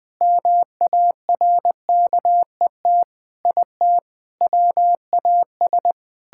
Morse code, heard as a keyed tone, about 20 wpm, 700 Hz. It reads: MARKET IT WAS
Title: Sentences from Top 1600 Words 20wpm